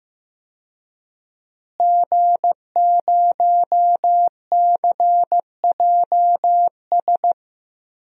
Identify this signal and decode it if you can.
Morse code, heard as a keyed tone, about 15 wpm, 700 Hz. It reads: G0CJS
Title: DXCC-100 Call Signs - 5 Characters - Repeated 2x - 15wpm